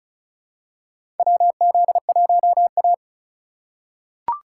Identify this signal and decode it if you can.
Morse code, heard as a keyed tone, about 34 wpm, 700 Hz. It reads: WZ1A
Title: USA Call Signs - 4 Characters - 2x Character Spacing - 35wpm